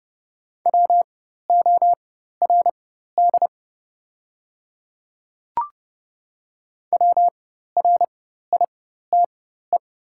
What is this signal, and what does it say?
Morse code, heard as a keyed tone, about 30 wpm, spaced as for 14 wpm, 700 Hz. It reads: WORD WRITE
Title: 4x Character Spacing - Top 100 Words - 30wpm